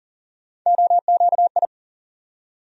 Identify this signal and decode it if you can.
Morse code, heard as a keyed tone, about 40 wpm, 700 Hz. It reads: OQI
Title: USA Call Sign Postfixes - 3 Letters 40wpm